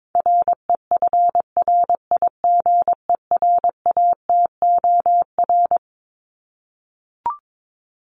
Morse code, heard as a keyed tone, about 22 wpm, 700 Hz. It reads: REFRIGERATOR
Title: Top 1501-1750 Words 22wpm